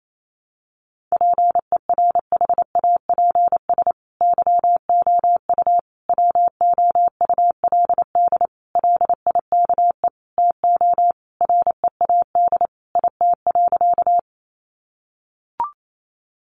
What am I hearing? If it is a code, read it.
Morse code, heard as a keyed tone, about 28 wpm, 700 Hz. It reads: PERHAPS YOU WOULD LIKE TO READ IT.
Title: Sentences from Top 500 Words 28wpm